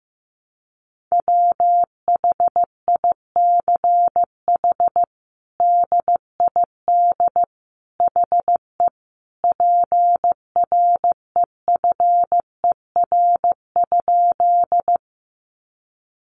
Morse code, heard as a keyed tone, about 15 wpm, 700 Hz. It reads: WHICH DID HE PREFER?
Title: Sentences from Top 1100 Words 15wpm